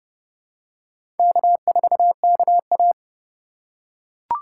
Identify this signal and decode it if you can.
Morse code, heard as a keyed tone, about 30 wpm, 700 Hz. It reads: K4KA
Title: USA Call Signs - 4 Characters - 3x Character Spacing - 30wpm